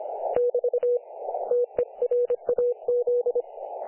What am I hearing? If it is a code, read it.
Morse code, heard as a keyed tone, about 25 wpm, 500 Hz. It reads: <BT> TERAZ
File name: dlazaawansowanych / nagraniazpasma / websdr recording 3564.7kHz